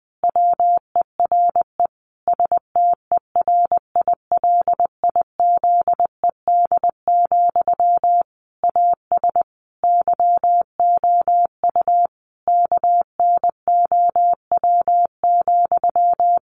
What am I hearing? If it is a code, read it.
Morse code, heard as a keyed tone, about 20 wpm, 700 Hz. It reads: WERE STERILIZED, AS YOU KNOW,
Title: A Sound Of Thunder 20wpm